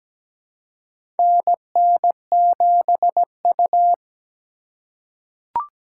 Morse code, heard as a keyed tone, about 17 wpm, 700 Hz. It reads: NN7U
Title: USA Call Signs - 4 Characters - 3x Character Spacing - 17wpm